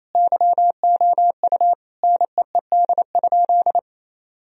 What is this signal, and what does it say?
Morse code, heard as a keyed tone, about 28 wpm, 700 Hz. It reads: YOU NEED?
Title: Sentences from Top 1000 Words - Review Part 4 - 28wpm